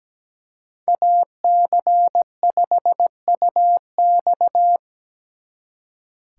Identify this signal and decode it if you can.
Morse code, heard as a keyed tone, about 19 wpm, 700 Hz. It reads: AC5UX
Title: Speed Racing - USA 2x2 Call Signs 17wpm